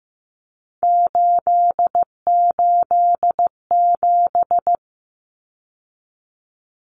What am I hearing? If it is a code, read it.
Morse code, heard as a keyed tone, about 15 wpm, 700 Hz. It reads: 887
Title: Numbers - 3-Digit 15wpm